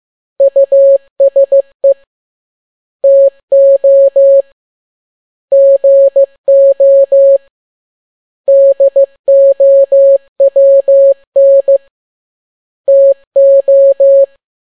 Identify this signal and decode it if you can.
Morse code, heard as a keyed tone, about 15 wpm, 550 Hz. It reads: USE TO GO DOWN TO
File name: n6tt p1 15C13